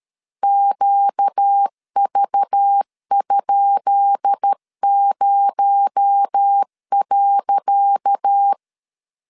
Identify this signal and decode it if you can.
Morse code, heard as a keyed tone, about 12 wpm, 800 Hz. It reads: QV?0.
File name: LEZIONI / 10 SIGNS / 10C DOT 12wpm 2451